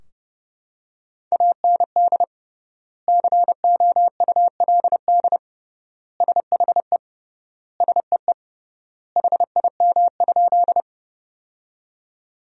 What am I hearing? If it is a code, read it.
Morse code, heard as a keyed tone, about 30 wpm, 700 Hz. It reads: AND COULD SHE SEE HIM?
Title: 3x Word-Spacing - Sentences from Top 100 Words - 30wpm